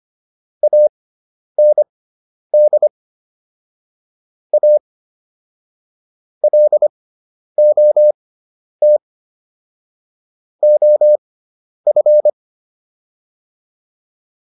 Morse code, heard as a keyed tone, about 25 wpm, spaced as for 10 wpm, 600 Hz. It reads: AND A LOT OF